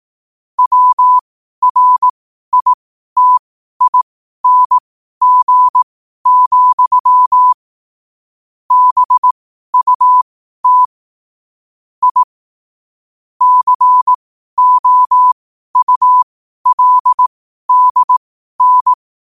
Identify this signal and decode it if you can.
Morse code, heard as a keyed tone, about 18 wpm, spaced as for 12 wpm, 1000 Hz. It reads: WRITING, BUT I COULDN
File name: qotdfiles / 12wpm